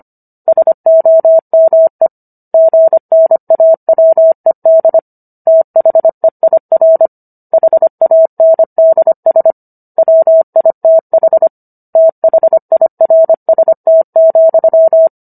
Morse code, heard as a keyed tone, about 25 wpm, 650 Hz. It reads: SOME GNAWED THEIR HANDS WITH THIRST,